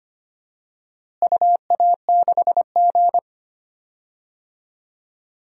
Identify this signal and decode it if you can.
Morse code, heard as a keyed tone, about 25 wpm, 700 Hz. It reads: UA6G
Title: DXCC-100 Call Signs - 4 Characters - Repeated 2x - 25wpm